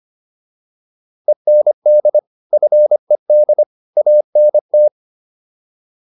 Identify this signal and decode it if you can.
Morse code, heard as a keyed tone, about 25 wpm, 600 Hz. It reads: END FED ANT